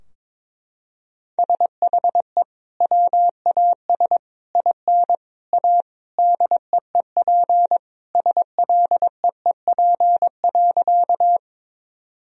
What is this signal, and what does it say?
Morse code, heard as a keyed tone, about 22 wpm, 700 Hz. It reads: SHE WAS IN A DEEP SLEEP.
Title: Sentences from Top 1000 Words - Review Part 3 - 22wpm